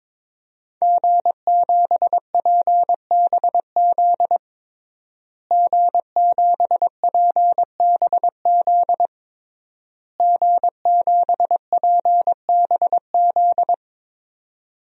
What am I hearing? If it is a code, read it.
Morse code, heard as a keyed tone, about 22 wpm, 700 Hz. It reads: G7PBZ G7PBZ G7PBZ
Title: DXCC-100 Call Signs - 5 Characters - Repeated 3x - 22wpm